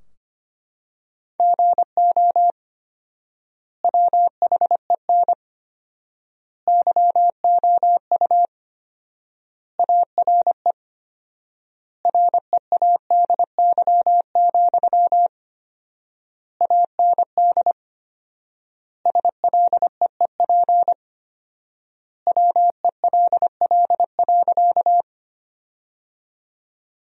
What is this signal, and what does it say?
Morse code, heard as a keyed tone, about 25 wpm, 700 Hz. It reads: GO WHEN YOU ARE READY, AND SLEEP WELL.
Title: 4x Word-Spacing - Sentences from Top 500 Words - 25wpm